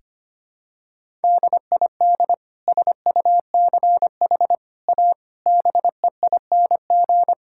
Morse code, heard as a keyed tone, about 25 wpm, 700 Hz. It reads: DID SUCH A BEING
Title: Sentences from Top 1000 Words - Review Part 5 - 25wpm